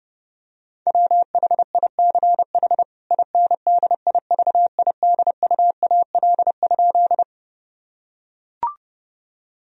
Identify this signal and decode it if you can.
Morse code, heard as a keyed tone, about 32 wpm, 700 Hz. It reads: WHICH INDIVIDUAL?
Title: Sentences from Top 1000 Words - Review Part 4 - 30wpm